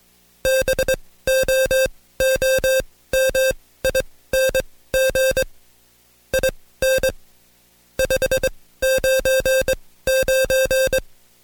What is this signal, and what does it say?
Morse code, heard as a keyed tone, about 21 wpm, spaced as for 15 wpm, 530 Hz. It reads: BOOMING IN 599
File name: Code Quick / AudioDownld2 / Seg134-Stepper7 15WPM